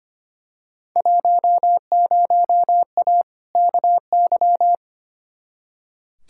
Morse code, heard as a keyed tone, about 25 wpm, 700 Hz. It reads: 10A KY
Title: Field Day Exchanges Encore - 25wpm